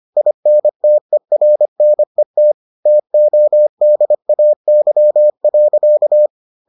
Morse code, heard as a keyed tone, about 25 wpm, 600 Hz. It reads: INTERNET TODAY.